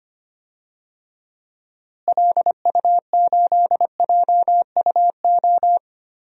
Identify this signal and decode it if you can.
Morse code, heard as a keyed tone, about 25 wpm, 700 Hz. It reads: LU8JUO
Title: DXCC-100 Call Signs - 6 Characters - Repeated 3x - 25wpm